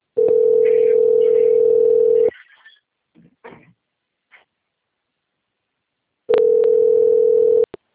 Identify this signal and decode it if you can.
Ringback tone (North America), 440 + 480 Hz.